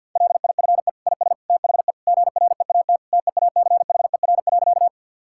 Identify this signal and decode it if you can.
Morse code, heard as a keyed tone, about 47 wpm, 700 Hz. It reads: LIFE IS THE GREAT TEACHER.
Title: Sentences from Top 1000 Words - Review Part 6 - 50wpm